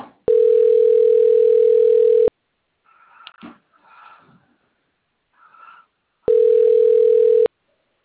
Ringback tone (North America), 440 + 480 Hz.